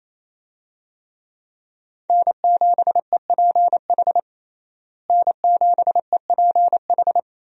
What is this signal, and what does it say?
Morse code, heard as a keyed tone, about 28 wpm, 700 Hz. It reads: N7EPH N7EPH
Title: USA Call Signs - 5 Characters - Repeated 2x - 28wpm